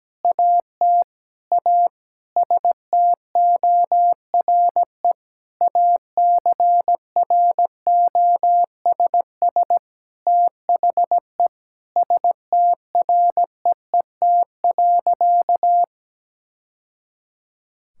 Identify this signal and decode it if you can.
Morse code, heard as a keyed tone, about 17 wpm, 700 Hz. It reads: AT A STORE ACROSS THE STREET.
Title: Sentences from Top 1000 Words - Review Part 5 - 17wpm